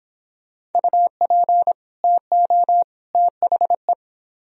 Morse code, heard as a keyed tone, about 31 wpm, 700 Hz. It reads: UP TO THE